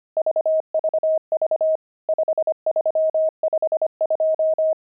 Morse code, heard as a keyed tone, about 25 wpm, 630 Hz. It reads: VVV 5352
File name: pangram-fs-25-cs-25-nums5